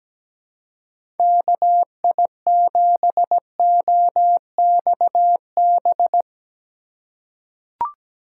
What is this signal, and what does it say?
Morse code, heard as a keyed tone, about 17 wpm, 700 Hz. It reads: KI7OXB
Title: USA Call Signs - 6 Characters - 6x Character Spacing - 17wpm